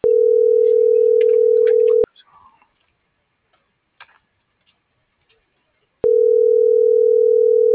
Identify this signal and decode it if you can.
Ringback tone (North America), 440 + 480 Hz.